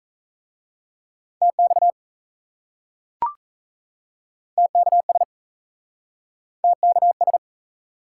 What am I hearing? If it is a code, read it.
Morse code, heard as a keyed tone, about 36 wpm, 700 Hz. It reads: TX TKS TKS